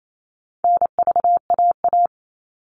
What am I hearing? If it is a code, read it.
Morse code, heard as a keyed tone, about 28 wpm, 700 Hz. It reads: NVAA
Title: All Cut Numbers - 4-Digit 28wpm